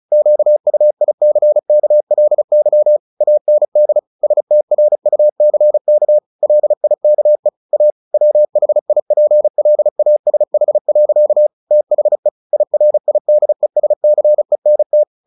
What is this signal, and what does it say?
Morse code, heard as a keyed tone, about 34 wpm, 600 Hz. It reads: QUICKLY AND STRUCK LIKE A WHIPLASH. THE IRIDESCENT
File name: the old man and the sea / the old man and the sea 1 35wpm